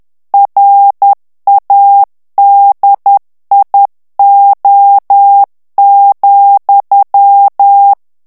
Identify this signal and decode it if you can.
Morse code, heard as a keyed tone, about 10 wpm, 800 Hz. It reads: RADIO,
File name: ANGLAIS texte clair 3 vitesse 10 mots minute